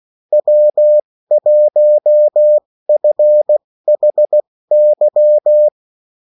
Morse code, heard as a keyed tone, about 16 wpm, spaced as for 14 wpm, 600 Hz. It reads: W1FHY
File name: morse mp3 / revision9-vitesse 14 mots